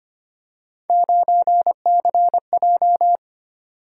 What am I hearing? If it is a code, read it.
Morse code, heard as a keyed tone, about 25 wpm, 700 Hz. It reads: 9CJ
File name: Three Letter-Number Combinations - 25wpm